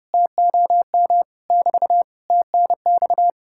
Morse code, heard as a keyed tone, about 40 wpm, 700 Hz. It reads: TOM <BT> TNX